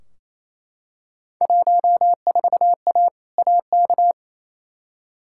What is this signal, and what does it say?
Morse code, heard as a keyed tone, about 28 wpm, 700 Hz. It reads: 14A AK
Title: Field Day Exchanges Encore - 28wpm